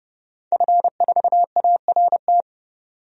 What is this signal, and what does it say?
Morse code, heard as a keyed tone, about 30 wpm, 700 Hz. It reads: F4ART
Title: DXCC-100 Call Signs - 5 Characters - Repeated 2x - 30wpm